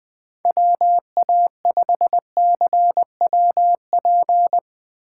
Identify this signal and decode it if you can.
Morse code, heard as a keyed tone, about 19 wpm, 700 Hz. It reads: WA5CWP
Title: Speed Racing - USA Call Signs 17wpm